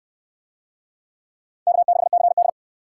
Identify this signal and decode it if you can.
Morse code, heard as a keyed tone, about 64 wpm, 700 Hz. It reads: K6CD